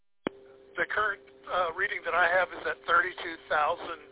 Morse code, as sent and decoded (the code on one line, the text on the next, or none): none